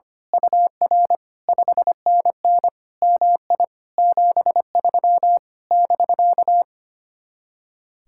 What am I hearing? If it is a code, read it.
Morse code, heard as a keyed tone, about 24 wpm, 700 Hz. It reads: UR 5NN MI 73 <BK>
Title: Speed-Racing Common QSO Phrases - 25wpm